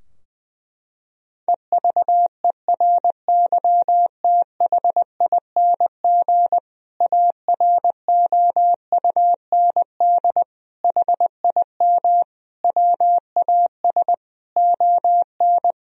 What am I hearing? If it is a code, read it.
Morse code, heard as a keyed tone, about 20 wpm, 700 Hz. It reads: EVERYTHING AROUND HIM WAS ON
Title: Sentences from Top 1000 Words - Review Part 3 - 20wpm